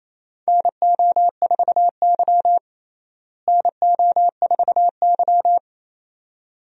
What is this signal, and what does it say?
Morse code, heard as a keyed tone, about 27 wpm, 700 Hz. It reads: NO4Y NO4Y
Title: USA Call Signs - 4 Characters - Repeated 2x - 28wpm